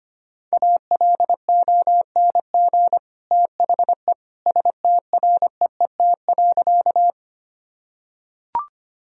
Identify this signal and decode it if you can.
Morse code, heard as a keyed tone, about 25 wpm, 700 Hz. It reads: ALONG THE STREET.
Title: Sentences from Top 1000 Words - Review Part 3 - 25wpm